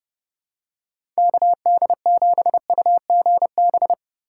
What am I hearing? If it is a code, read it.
Morse code, heard as a keyed tone, about 32 wpm, 700 Hz. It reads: KD7UGB
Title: USA Call Signs - 6 Characters - 2x Character Spacing - 30wpm